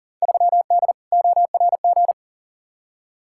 Morse code, heard as a keyed tone, about 40 wpm, 700 Hz. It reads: 3D ORG
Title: Field Day Exchanges Encore - 40wpm